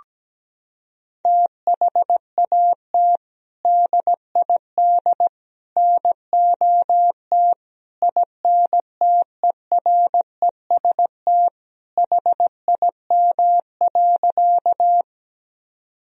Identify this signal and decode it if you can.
Morse code, heard as a keyed tone, about 17 wpm, 700 Hz. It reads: THAT DID NOT INTEREST HIM.
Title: Sentences from Top 1000 Words - Review Part 2 - 17wpm